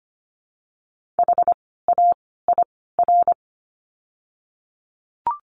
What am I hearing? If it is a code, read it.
Morse code, heard as a keyed tone, about 25 wpm, 700 Hz. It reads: H A I R
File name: course / 25:10wpm / 094 Lesson 16f - Letter c - Words 25wpm